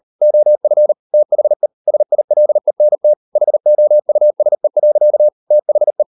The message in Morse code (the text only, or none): OF THE SILENT HOUSE. THE